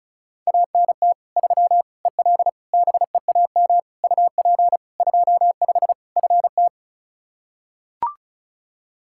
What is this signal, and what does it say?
Morse code, heard as a keyed tone, about 36 wpm, 700 Hz. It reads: ANT 3 EL BEAM UP 25 FT
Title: K1USN SST QSO Practice 35wpm